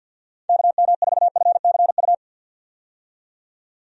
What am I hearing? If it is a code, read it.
Morse code, heard as a keyed tone, about 52 wpm, 700 Hz. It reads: KM4FCU
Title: USA Call Signs - 6 Characters - 50wpm